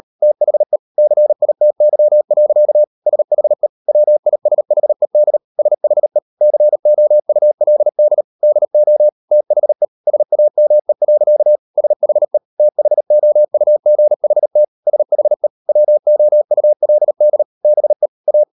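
Morse code, heard as a keyed tone, about 38 wpm, 600 Hz. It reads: THE CITY. SHE WISHED SHE COULD DO THE SAME. SHE THOUGHT SHE WOULD BE A
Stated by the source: Morse code practice at 38 WPM, sending Ken Follett's novel Never in English, from CW med Gnister.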